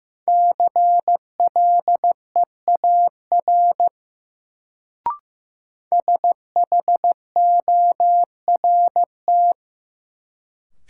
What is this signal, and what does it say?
Morse code, heard as a keyed tone, about 15 wpm, 700 Hz. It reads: CLEAR SHORT
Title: Sets of 5 Words Encore - New Word Drill - 15wpm